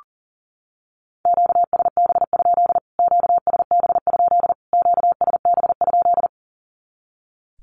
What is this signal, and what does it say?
Morse code, heard as a keyed tone, about 40 wpm, 700 Hz. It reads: QSB? QSB? QSB?
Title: Q-Codes Repeated 3x 40wpm